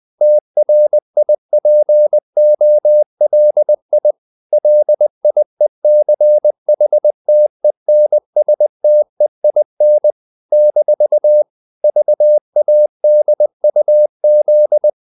Morse code, heard as a keyed tone, about 20 wpm, 600 Hz. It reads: TRIPOLI LIECHTENSTEIN - VADUZ
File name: CW / Practice / NationalCapitals20